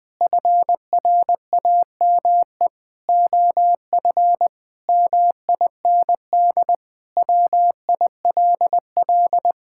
Morse code, heard as a keyed tone, about 21 wpm, 700 Hz. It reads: FRAME OF MIND WILL
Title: Sentences from Top 1000 Words - Review Part 6 - 20wpm